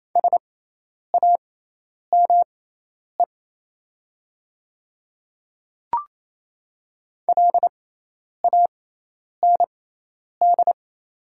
Morse code, heard as a keyed tone, about 28 wpm, spaced as for 10 wpm, 700 Hz. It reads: SAME LAND
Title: 6x Character Spacing - Top 500 Words - 28wpm